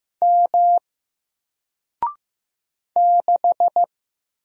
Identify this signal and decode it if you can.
Morse code, heard as a keyed tone, about 15 wpm, 700 Hz. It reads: M 6
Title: Single Letter-Number Alt 15wpm